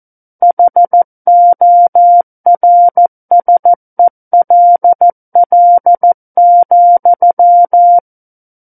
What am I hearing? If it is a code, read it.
Morse code, heard as a keyed tone, about 14 wpm, 700 Hz. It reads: HORSELL,
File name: Book One / 05-Chapter 5 - 14 WPM